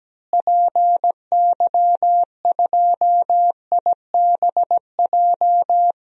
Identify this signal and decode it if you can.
Morse code, heard as a keyed tone, about 17 wpm, 700 Hz. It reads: PY2IBJ
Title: DXCC-100 Call Signs - 6 Characters - Repeated 3x - 17wpm